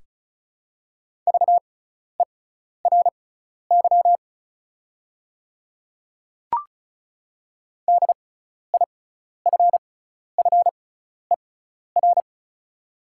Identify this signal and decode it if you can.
Morse code, heard as a keyed tone, about 34 wpm, spaced as for 12 wpm, 690 Hz. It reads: VERY DIFFER
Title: 6x Character Spacing - Top 300 Words - 35wpm